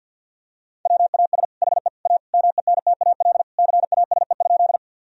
Morse code, heard as a keyed tone, about 50 wpm, 700 Hz. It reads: WAS HE A MENTAL CASE?
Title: Sentences from Top 1000 Words - Review Part 4 - 50wpm